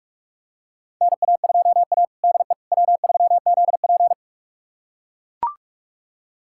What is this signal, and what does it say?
Morse code, heard as a keyed tone, about 42 wpm, 700 Hz. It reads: NA2A DE W3ZP
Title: USA Callsign DE Callsign - 4 Characters - 45wpm